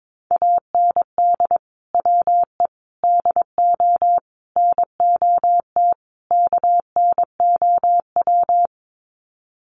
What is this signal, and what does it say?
Morse code, heard as a keyed tone, about 22 wpm, 700 Hz. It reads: AND WE DO NOT KNOW
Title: Sets of 5 Words Encore 22wpm